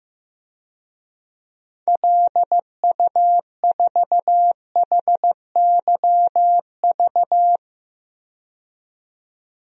Morse code, heard as a keyed tone, about 15 wpm, 700 Hz. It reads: LU4HYV
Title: DXCC-100 Call Signs - 6 Characters - 15wpm